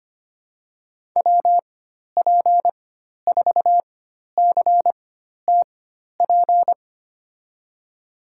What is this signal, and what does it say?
Morse code, heard as a keyed tone, about 25 wpm, spaced as for 12 wpm, 700 Hz. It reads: WP4CTP
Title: USA Call Signs - 6 Characters - 4x Character Spacing - 25wpm